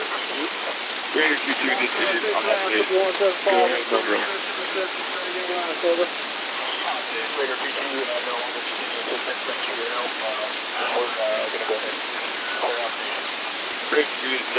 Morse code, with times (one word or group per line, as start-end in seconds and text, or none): none